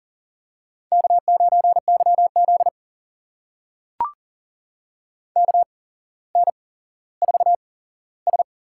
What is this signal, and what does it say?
Morse code, heard as a keyed tone, about 40 wpm, spaced as for 17 wpm, 700 Hz. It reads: K9YZ KN4S
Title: USA Call Signs - 4 Characters - 8x Character Spacing - 40wpm